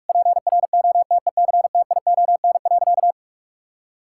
Morse code, heard as a keyed tone, about 46 wpm, 700 Hz. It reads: PROTECTION.
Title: Sentences from Top 1600 Words 45wpm